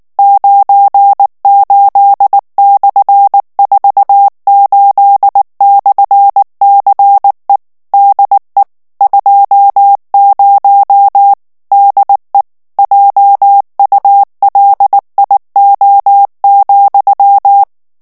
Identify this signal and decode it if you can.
Morse code, heard as a keyed tone, about 19 wpm, 800 Hz. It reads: 98/48/CE DE 20 DE JULIO,
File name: morse mp3 / ESPAGNOL TEXTE EN CLAIR 2 VITESSE 18 MOTS MINUTES